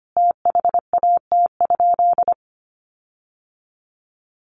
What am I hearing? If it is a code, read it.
Morse code, heard as a keyed tone, about 25 wpm, 700 Hz. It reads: THAT?
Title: Sentences from Top 1400 Words 25wpm